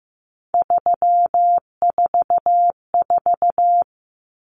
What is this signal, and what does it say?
Morse code, heard as a keyed tone, about 15 wpm, 700 Hz. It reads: 344
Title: Numbers - 3-Digit 15wpm